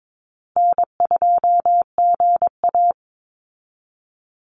Morse code, heard as a keyed tone, about 22 wpm, 700 Hz. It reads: N2GA
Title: Easy Call Signs Repeated 3x - 22wpm